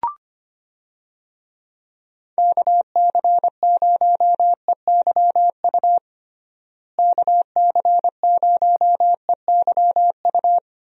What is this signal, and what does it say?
Morse code, heard as a keyed tone, about 25 wpm, 700 Hz. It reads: KC0EYU KC0EYU
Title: USA Call Signs - 6 Characters - Repeated 3x - 25wpm